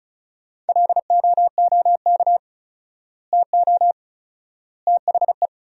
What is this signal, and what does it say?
Morse code, heard as a keyed tone, about 34 wpm, 700 Hz. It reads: LOOK TO THE
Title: 4x Word-Spacing - Sentences from Top 400 Words - 35wpm